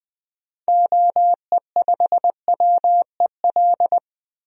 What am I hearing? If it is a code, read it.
Morse code, heard as a keyed tone, about 20 wpm, 700 Hz. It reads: OE5WEL
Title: DXCC-100 Call Signs - 6 Characters - 20wpm